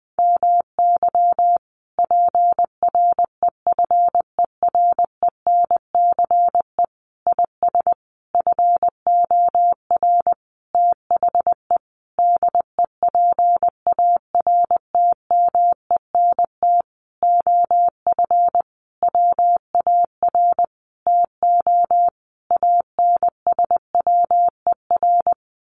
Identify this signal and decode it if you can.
Morse code, heard as a keyed tone, about 20 wpm, 700 Hz. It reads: MY PREFERENCE IS FOR THE DEPARTMENT OF WAR TO ANSWER
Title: Sentences from Top 1500 Words 20wpm